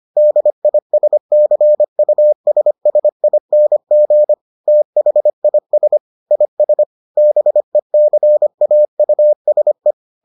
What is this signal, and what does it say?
Morse code, heard as a keyed tone, about 25 wpm, 600 Hz. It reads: DISCUSSING THIS IS BECAUSE